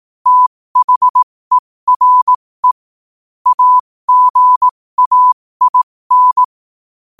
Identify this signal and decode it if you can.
Morse code, heard as a keyed tone, about 18 wpm, spaced as for 15 wpm, 1000 Hz. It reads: THERE AGAIN
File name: qotdfiles / 15wpm